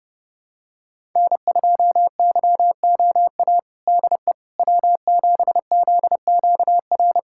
Morse code, heard as a keyed tone, about 31 wpm, 700 Hz. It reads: N2YOA DE W7ZQR
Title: USA Callsign DE Callsign - 5 Characters - 30wpm